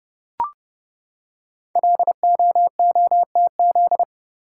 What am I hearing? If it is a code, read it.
Morse code, heard as a keyed tone, about 28 wpm, 700 Hz. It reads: LOOTZ
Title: Code Groups - 5 Letters - 30wpm